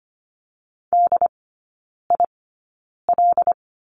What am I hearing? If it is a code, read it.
Morse code, heard as a keyed tone, about 25 wpm, spaced as for 9 wpm, 690 Hz. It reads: DIL
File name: Lesson 15e - Number 5 - Three Characters 25wpm